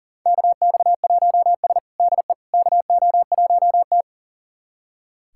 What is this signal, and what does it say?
Morse code, heard as a keyed tone, about 37 wpm, 700 Hz. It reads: KX1S DE KO1T
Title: USA Callsign DE Callsign - 4 Characters - 40wpm